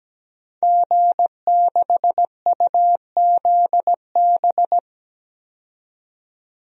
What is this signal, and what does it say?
Morse code, heard as a keyed tone, about 17 wpm, 700 Hz. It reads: G6UZB
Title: DXCC-100 Call Signs - 5 Characters - 17wpm